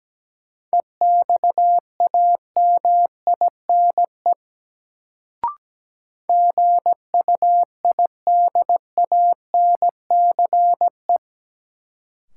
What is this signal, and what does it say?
Morse code, heard as a keyed tone, about 17 wpm, 700 Hz. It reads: EXAMINE GUIDANCE
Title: Sentences from Top 1600 Words - New Word Drill Alt - 17wpm